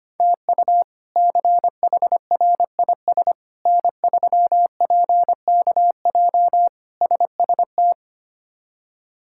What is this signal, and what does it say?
Morse code, heard as a keyed tone, about 25 wpm, 700 Hz. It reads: TU CHRIS N3PKJ SST
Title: K1USN SST QSO Practice - 25wpm